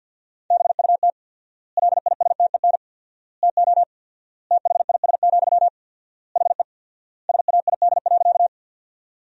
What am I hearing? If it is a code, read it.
Morse code, heard as a keyed tone, about 50 wpm, 700 Hz. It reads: BUT LISTEN TO THIS, HE SAID.
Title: 4x Word-Spacing - Sentences from Top 400 Words - 50wpm